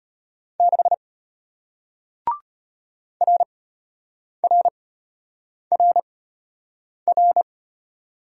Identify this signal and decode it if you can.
Morse code, heard as a keyed tone, about 34 wpm, spaced as for 8 wpm, 700 Hz. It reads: 6 RRRR